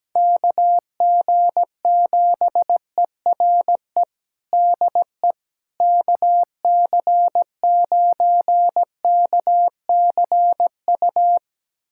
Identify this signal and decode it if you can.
Morse code, heard as a keyed tone, about 17 wpm, 700 Hz. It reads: KG7ERE DE KC9KCU
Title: USA Callsign DE Callsign 17wpm